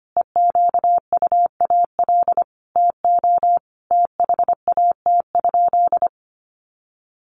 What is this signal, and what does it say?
Morse code, heard as a keyed tone, about 25 wpm, 700 Hz. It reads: EQUAL TO THAT?
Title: Sentences from Top 1400 Words 25wpm